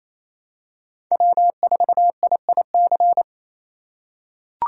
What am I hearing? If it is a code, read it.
Morse code, heard as a keyed tone, about 27 wpm, 700 Hz. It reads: W4IIC
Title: K1USN SST QSO Practice - 28wpm